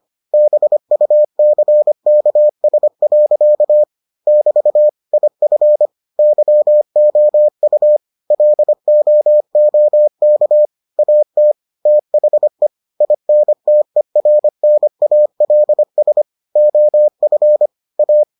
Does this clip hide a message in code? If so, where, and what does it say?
Yes: Morse code from 0.3 to 18.2 s, heard as a keyed tone, about 25 wpm, 600 Hz. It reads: BUCKS. <BT> IF YOU LOOK AT THE INTERNALS OF A